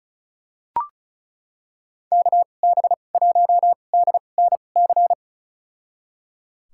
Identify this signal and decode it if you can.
Morse code, heard as a keyed tone, about 36 wpm, spaced as for 25 wpm, 700 Hz. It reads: KB1DNC
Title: USA Call Signs - 6 Characters - 2x Character Spacing - 35wpm